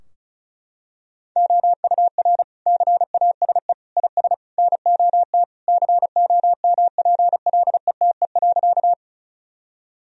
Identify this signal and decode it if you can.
Morse code, heard as a keyed tone, about 34 wpm, 700 Hz. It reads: OUR CASE IS NOT COMPLETE.
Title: Sentences from Top 1000 Words - Review Part 2 - 35wpm